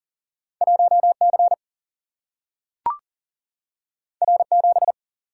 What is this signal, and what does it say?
Morse code, heard as a keyed tone, about 39 wpm, 700 Hz. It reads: 1 C R 7
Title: Single-Letter Single-Number 40wpm